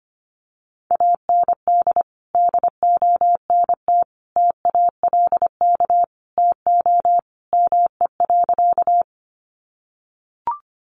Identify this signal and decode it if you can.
Morse code, heard as a keyed tone, about 25 wpm, 700 Hz. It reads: AND DONT TALK TO ME.
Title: Sentences from Top 400 Words 25wpm